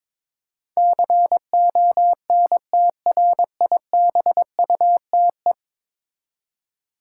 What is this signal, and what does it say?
Morse code, heard as a keyed tone, about 22 wpm, 700 Hz. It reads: CONTRIBUTE
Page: morse-code-ninja-practice-22wpm